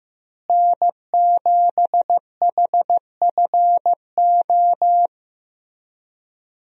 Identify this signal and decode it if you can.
Morse code, heard as a keyed tone, about 15 wpm, 700 Hz. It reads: N7HFO
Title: USA Call Signs - 5 Characters - Repeated 3x - 15wpm